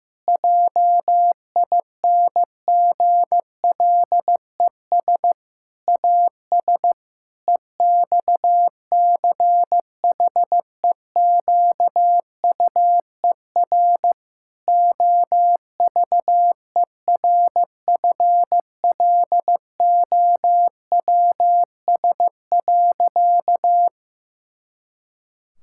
Morse code, heard as a keyed tone, about 15 wpm, 700 Hz. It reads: JINGLES AS EXCHEQUER OVERFLOWS.
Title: Pangram Challenge v2 - 15wpm